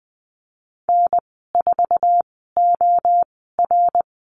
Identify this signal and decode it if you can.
Morse code, heard as a keyed tone, about 20 wpm, spaced as for 15 wpm, 700 Hz. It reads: N4OR
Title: 069 Lesson 12g - Letter d - Call Signs 20wpm